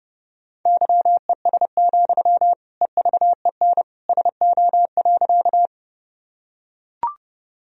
Morse code, heard as a keyed tone, about 30 wpm, 700 Hz. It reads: YES, EVEN SO.
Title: Sentences from Top 1000 Words - Review Part 1 - 30wpm